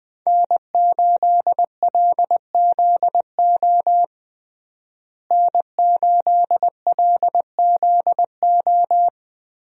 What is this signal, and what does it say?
Morse code, heard as a keyed tone, about 20 wpm, 700 Hz. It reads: N8LZO N8LZO
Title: USA Call Signs - 5 Characters - Repeated 3x - 20wpm